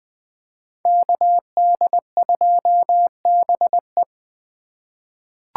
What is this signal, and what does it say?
Morse code, heard as a keyed tone, about 20 wpm, 700 Hz. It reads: KD2BE
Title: USA Call Signs - 5 Characters - 3x Character Spacing - 20wpm